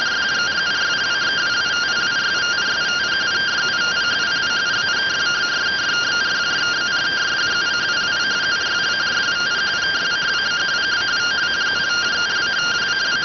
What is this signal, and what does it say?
Two-tone teleprinter signal, about 170 Hz shift.